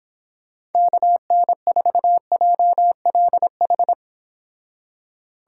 Morse code, heard as a keyed tone, about 31 wpm, 700 Hz. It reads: KN4JLH